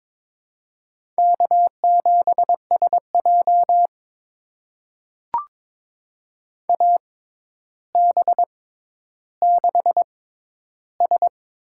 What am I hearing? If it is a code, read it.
Morse code, heard as a keyed tone, about 22 wpm, spaced as for 11 wpm, 700 Hz. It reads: K7SJ AB6S
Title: USA Call Signs - 4 Characters - 6x Character Spacing - 22wpm